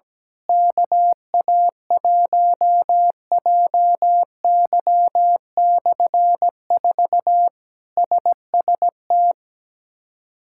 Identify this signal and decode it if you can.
Morse code, heard as a keyed tone, about 17 wpm, 700 Hz. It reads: KA1JY/4 SST
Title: K1USN SST QSO Practice - 17wpm